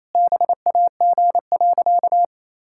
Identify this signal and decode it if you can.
Morse code, heard as a keyed tone, about 28 wpm, 700 Hz. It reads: BAG.
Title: Pangram Challenge v2 - 28wpm